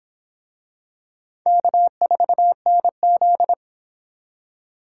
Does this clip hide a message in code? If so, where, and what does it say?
Yes: Morse code from 1.5 to 3.5 s, heard as a keyed tone, about 23 wpm, 700 Hz. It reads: K4NZ